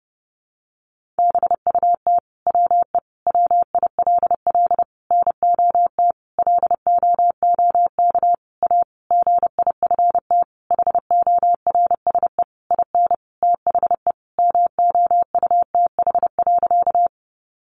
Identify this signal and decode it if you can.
Morse code, heard as a keyed tone, about 30 wpm, 700 Hz. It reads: BUT WE WILL NOT LOOK A GIFT HORSE IN THE MOUTH.
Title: Sentences from Top 1500 Words 30wpm